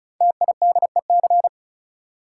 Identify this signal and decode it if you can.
Morse code, heard as a keyed tone, about 36 wpm, 690 Hz. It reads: TIDEC
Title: Code Groups - 5 Letters - 35wpm